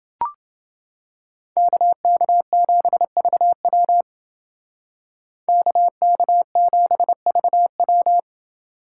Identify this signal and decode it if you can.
Morse code, heard as a keyed tone, about 29 wpm, 700 Hz. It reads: KK7VW KK7VW